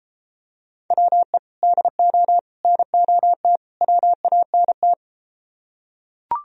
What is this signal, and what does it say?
Morse code, heard as a keyed tone, about 33 wpm, 700 Hz. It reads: WE DO NOT WANT